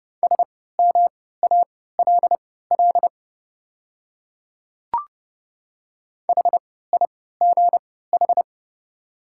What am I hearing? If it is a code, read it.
Morse code, heard as a keyed tone, about 30 wpm, spaced as for 17 wpm, 700 Hz. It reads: SMALL HIGH
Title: 3x Character Spacing - Top 500 Words - 30wpm